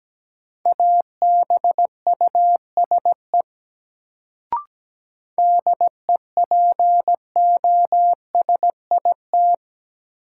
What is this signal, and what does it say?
Morse code, heard as a keyed tone, about 17 wpm, 700 Hz. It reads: ABUSE DEPOSIT
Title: Sentences from Top 1300 Words - New Word Drill Alt - 17wpm